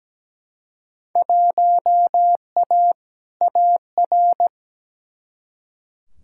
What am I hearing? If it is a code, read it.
Morse code, heard as a keyed tone, about 17 wpm, 700 Hz. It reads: 1A AR
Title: Field Day Exchanges Encore - 17wpm